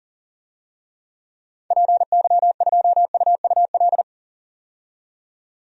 Morse code, heard as a keyed tone, about 40 wpm, 700 Hz. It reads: PY2UUL
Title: DXCC-100 Call Signs - 6 Characters - 40wpm